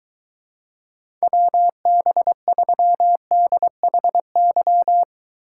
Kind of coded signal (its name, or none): Morse code